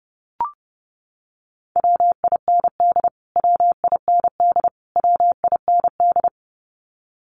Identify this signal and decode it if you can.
Morse code, heard as a keyed tone, about 30 wpm, 700 Hz. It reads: WIND WIND WIND
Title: Top 500 Words Repeated 30wpm